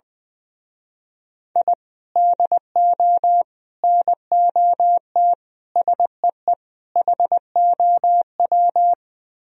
Morse code, heard as a keyed tone, about 22 wpm, 700 Hz. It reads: I DO NOT SEE HOW
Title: Speed-Racing - Sets of 5 Words 20wpm